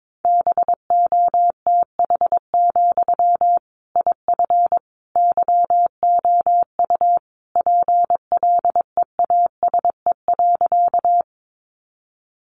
Morse code, heard as a keyed tone, about 22 wpm, 700 Hz. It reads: BOTH, IF YOU PLEASE.
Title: Sentences from Top 1500 Words 22wpm